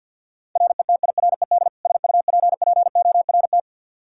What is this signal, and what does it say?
Morse code, heard as a keyed tone, about 53 wpm, 700 Hz. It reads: RETIRED SUPPORT
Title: K1USN SST QSO Practice 50wpm